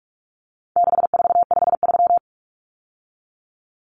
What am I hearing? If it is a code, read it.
Morse code, heard as a keyed tone, about 44 wpm, 700 Hz. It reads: 6453
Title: Numbers - 4-Digit 45wpm